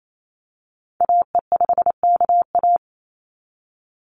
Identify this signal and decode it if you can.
Morse code, heard as a keyed tone, about 28 wpm, 700 Hz. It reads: AE5KA
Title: NA SOTA Activator Callsigns 28wpm